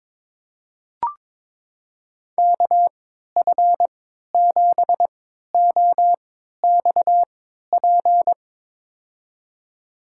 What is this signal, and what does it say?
Morse code, heard as a keyed tone, about 22 wpm, spaced as for 13 wpm, 700 Hz. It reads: KF7OXP
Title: USA Call Signs - 6 Characters - 3x Character Spacing - 22wpm